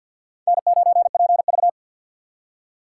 Morse code, heard as a keyed tone, about 51 wpm, 700 Hz. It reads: N9PV
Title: USA Call Signs - 4 Characters - 2x Character Spacing - 50wpm